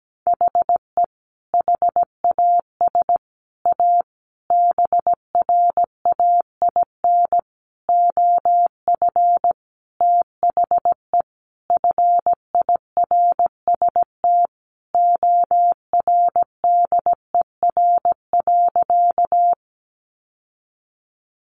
Morse code, heard as a keyed tone, about 17 wpm, 700 Hz. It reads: HE HAS A BRAIN OF THE FIRST ORDER.
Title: Sentences from Top 1100 Words 17wpm